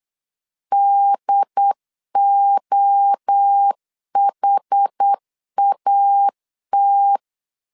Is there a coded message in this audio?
Yes: Morse code from 0.7 to 7.2 s, heard as a keyed tone, about 8 wpm, 800 Hz. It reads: DOHAT